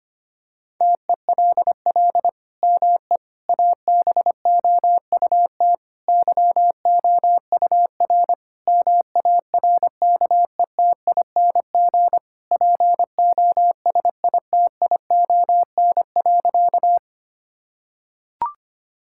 Morse code, heard as a keyed tone, about 25 wpm, 700 Hz. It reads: TELL ME ABOUT YOUR MARKETING POSITION.
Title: Sentences from Top 1000 Words - Review Part 5 - 25wpm